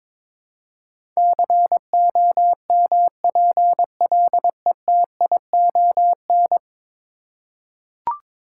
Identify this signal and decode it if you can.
Morse code, heard as a keyed tone, about 23 wpm, 700 Hz. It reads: COMPLETION
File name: practice / 22wpm / 10 Letter Words - First 3 Letters Given - 22wpm